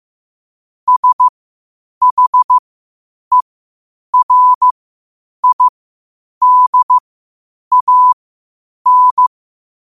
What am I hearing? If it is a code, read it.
Morse code, heard as a keyed tone, about 15 wpm, spaced as for 8 wpm, 1000 Hz. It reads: SHERIDAN
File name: qotdfiles / 8wpm